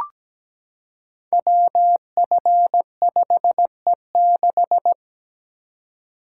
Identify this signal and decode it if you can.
Morse code, heard as a keyed tone, about 17 wpm, 700 Hz. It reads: WF5E6
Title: Code Groups - 5 Letter:Numbers - 17wpm